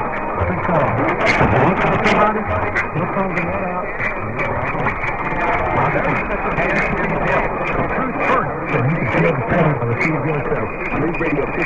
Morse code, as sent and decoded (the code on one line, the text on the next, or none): none